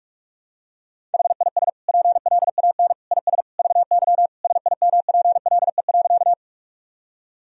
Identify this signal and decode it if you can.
Morse code, heard as a keyed tone, about 46 wpm, 700 Hz. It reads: HIS PLAN IS VY SIMPLE.
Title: Sentences from Top 500 Words and Abbreviations 45wpm